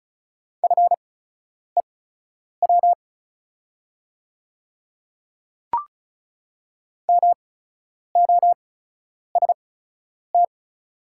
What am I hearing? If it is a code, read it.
Morse code, heard as a keyed tone, about 36 wpm, spaced as for 10 wpm, 690 Hz. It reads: FEW MOST
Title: 8x Character Spacing - Top 300 Words - 35wpm